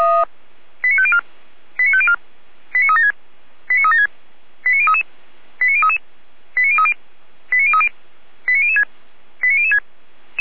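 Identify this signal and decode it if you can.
Five-tone selective call (ZVEI), heard as quick runs of five tones, tones of about 70 ms. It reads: 89504 89504 89376 89376 89033 89033 89030 89030 89006 89006
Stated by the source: ZVEI five-tone call received by SDR, from the server of feuerwehr-orlamuende.org.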